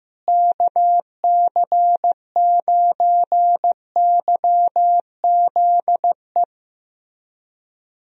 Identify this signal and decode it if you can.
Morse code, heard as a keyed tone, about 15 wpm, 700 Hz. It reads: KC9YZE
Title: USA Call Signs - 6 Characters - Repeated 3x - 15wpm